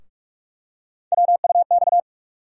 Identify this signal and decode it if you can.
Morse code, heard as a keyed tone, about 42 wpm, 700 Hz. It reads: WUX
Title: USA Call Sign Postfixes - 3 Letters 45wpm